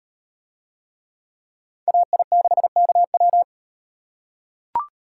Morse code, heard as a keyed tone, about 38 wpm, 700 Hz. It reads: AI6KW